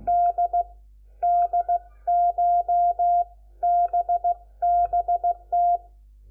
Morse code, heard as a keyed tone, about 16 wpm, spaced as for 13 wpm, 700 Hz. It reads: DDCHBBT
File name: Lecciones / CLASE Nº4